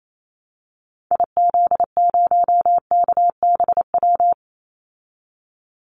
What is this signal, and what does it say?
Morse code, heard as a keyed tone, about 28 wpm, 700 Hz. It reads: IZ0KBW
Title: Call Signs Repeated 2x - 28wpm